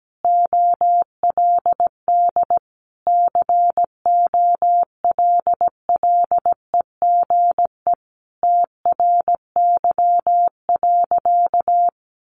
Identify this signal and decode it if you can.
Morse code, heard as a keyed tone, about 17 wpm, 700 Hz. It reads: OLD COLLEGE TRY.
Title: Sentences from Top 1300 Words 17wpm